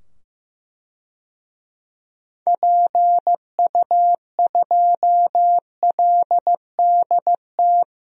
Morse code, heard as a keyed tone, about 15 wpm, 700 Hz. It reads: PU2LDT
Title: DXCC-100 Call Signs - 6 Characters - 15wpm